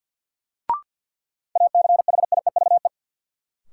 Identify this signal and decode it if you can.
Morse code, heard as a keyed tone, about 52 wpm, 700 Hz. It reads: ACHIEVE
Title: Sentences from Top 1000 Words - Review Part 5 - New Word Drill - 50wpm